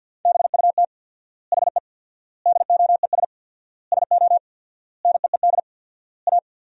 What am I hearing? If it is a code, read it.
Morse code, heard as a keyed tone, about 48 wpm, 700 Hz. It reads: BUT HE DOES SO NEED A
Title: 4x Word-Spacing - Sentences from Top 200 Words - 50wpm